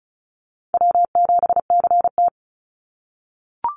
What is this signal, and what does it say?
Morse code, heard as a keyed tone, about 36 wpm, 700 Hz. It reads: W7CT
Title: Call Signs - 35wpm